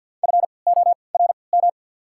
Morse code, heard as a keyed tone, about 52 wpm, spaced as for 29 wpm, 700 Hz. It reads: FORM
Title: 3x Character Spacing - Top 500 Words - 50wpm